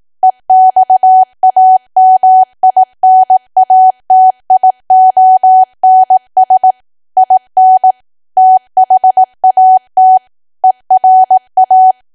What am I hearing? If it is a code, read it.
Morse code, heard as a keyed tone, about 18 wpm, 750 Hz. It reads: EXAMINATIONS IN THAT ERA
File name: cwa / W1AW-Level2 / 18WPM